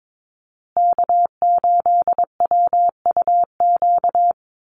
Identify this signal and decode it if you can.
Morse code, heard as a keyed tone, about 22 wpm, 700 Hz. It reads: K8WUQ
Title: USA Call Signs Repeated 3x 22wpm